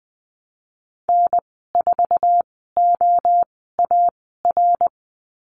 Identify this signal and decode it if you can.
Morse code, heard as a keyed tone, about 20 wpm, spaced as for 14 wpm, 700 Hz. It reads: N4OAR
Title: 076 Lesson 13g - Letter L - Call Signs 20wpm